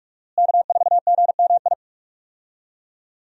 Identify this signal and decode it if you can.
Morse code, heard as a keyed tone, about 42 wpm, 700 Hz. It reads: K4GMI
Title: USA Call Signs - 5 Characters - Repeated 2x - 45wpm